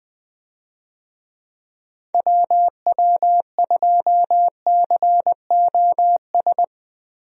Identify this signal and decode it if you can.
Morse code, heard as a keyed tone, about 20 wpm, 700 Hz. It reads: WW2COS
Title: USA Call Signs - 6 Characters - Repeated 3x - 20wpm